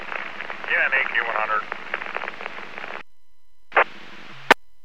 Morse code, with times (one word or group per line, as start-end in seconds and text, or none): none